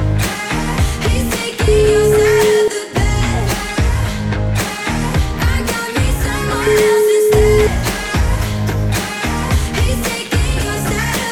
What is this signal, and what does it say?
Ringback tone (Europe, 425 Hz), with noise.